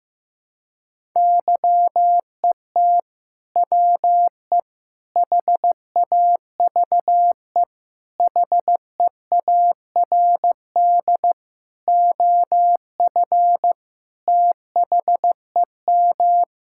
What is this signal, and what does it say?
Morse code, heard as a keyed tone, about 15 wpm, 700 Hz. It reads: YET WE HAVE HEARD OF THEM
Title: Sentences from Top 1000 Words - Review Part 5 - 15wpm